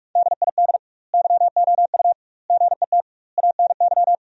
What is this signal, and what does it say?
Morse code, heard as a keyed tone, about 43 wpm, 700 Hz. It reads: DID YOU GET ANY
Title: 2x Word-Spacing - Sentences from Top 400 Words - 45wpm